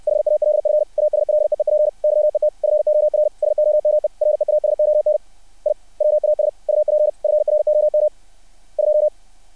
Morse code, heard as a keyed tone, about 60 wpm, 600 Hz. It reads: HALF ASLEEP YET FOR SOME REASON I WAS UP FROM 2
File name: archived / CRSnet